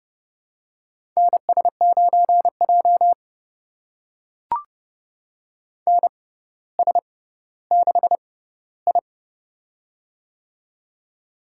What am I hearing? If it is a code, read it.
Morse code, heard as a keyed tone, about 30 wpm, spaced as for 15 wpm, 700 Hz. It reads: NS9J NS6I
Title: USA Call Signs - 4 Characters - 6x Character Spacing - 30wpm